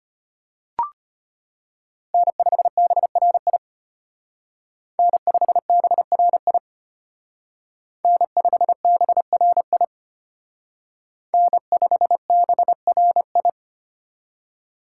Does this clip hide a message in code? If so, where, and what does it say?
Yes: Morse code from 2.1 to 13.5 s, heard as a keyed tone, about 33 wpm, 700 Hz. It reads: N5BRI N5BRI N5BRI N5BRI